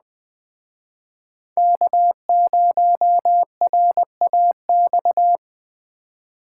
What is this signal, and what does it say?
Morse code, heard as a keyed tone, about 20 wpm, 700 Hz. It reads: K0RAX